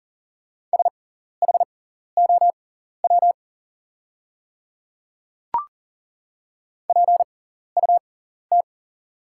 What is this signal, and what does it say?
Morse code, heard as a keyed tone, about 40 wpm, spaced as for 14 wpm, 690 Hz. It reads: SHOW PUT
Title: 6x Character Spacing - Top 200 Words - 40wpm